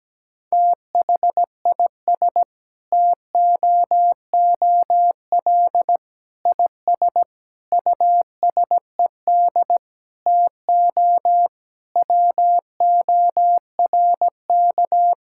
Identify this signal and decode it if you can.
Morse code, heard as a keyed tone, about 17 wpm, 700 Hz. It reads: THIS TOOL IS USED TO WORK
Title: Sentences from Top 1000 Words - Review Part 6 - 17wpm